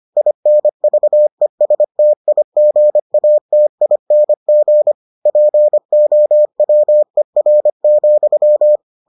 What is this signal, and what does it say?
Morse code, heard as a keyed tone, about 25 wpm, 600 Hz. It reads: INVESTIGATING POWER,